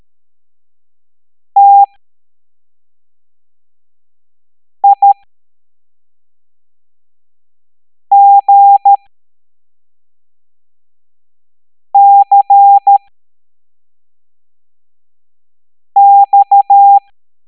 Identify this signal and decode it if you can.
Morse code, heard as a keyed tone, about 13 wpm, spaced as for 3 wpm, 800 Hz. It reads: TIGCX